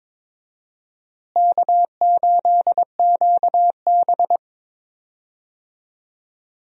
Morse code, heard as a keyed tone, about 22 wpm, 700 Hz. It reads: K8QB
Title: USA Call Signs - 4 Characters - Repeated 3x - 22wpm